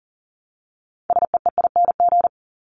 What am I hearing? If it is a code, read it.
Morse code, heard as a keyed tone, about 40 wpm, 700 Hz. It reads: SEEING
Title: ING Suffix - Top 100 Words with Base Words 40wpm